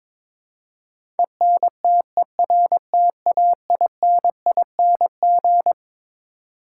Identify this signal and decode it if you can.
Morse code, heard as a keyed tone, about 22 wpm, 700 Hz. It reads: ENTERTAINING
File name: Twelve Letter Words - 22wpm